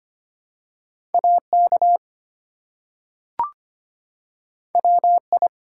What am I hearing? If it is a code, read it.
Morse code, heard as a keyed tone, about 25 wpm, 700 Hz. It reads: AK WI
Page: morse-code-ninja-practice-25wpm